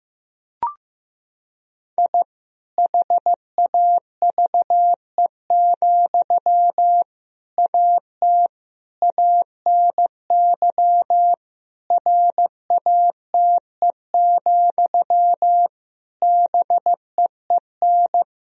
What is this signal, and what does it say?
Morse code, heard as a keyed tone, about 15 wpm, 700 Hz. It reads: I HAVE, AT ANY RATE, BEEN
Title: Sentences from Top 1000 Words - Review Part 4 - 15wpm